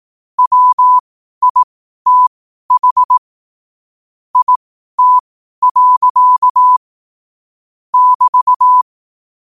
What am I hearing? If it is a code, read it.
Morse code, heard as a keyed tone, about 18 wpm, spaced as for 12 wpm, 1000 Hz. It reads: WITH IT. <BT>
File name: qotdfiles / 12wpm